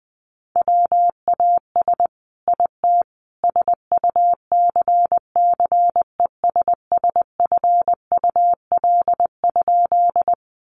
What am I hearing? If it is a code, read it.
Morse code, heard as a keyed tone, about 19 wpm, 700 Hz. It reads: WAS IT SUCCESSFUL?
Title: A Sound Of Thunder 20wpm